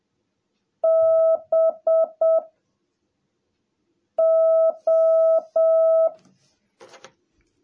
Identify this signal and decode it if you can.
Morse code, heard as a keyed tone, about 7 wpm, 660 Hz. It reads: B O
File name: jh8cbh / 2021cw / cw17a